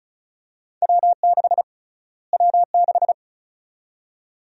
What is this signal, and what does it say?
Morse code, heard as a keyed tone, about 36 wpm, 700 Hz. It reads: W6 W6
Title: USA Call Sign Prefixes - Repeated 2x - 35wpm